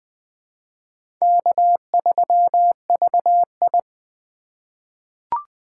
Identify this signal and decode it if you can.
Morse code, heard as a keyed tone, about 20 wpm, 700 Hz. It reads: K3VI
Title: USA Call Signs - 4 Characters - 3x Character Spacing - 20wpm